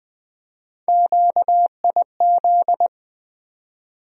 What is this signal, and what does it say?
Morse code, heard as a keyed tone, about 20 wpm, 700 Hz. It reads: QIZ
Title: USA Call Sign Postfixes - 3 Letters 20wpm